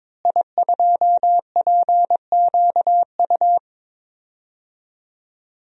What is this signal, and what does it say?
Morse code, heard as a keyed tone, about 22 wpm, 700 Hz. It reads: I2PQU
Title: DXCC-100 Call Signs - 5 Characters - Repeated 3x - 22wpm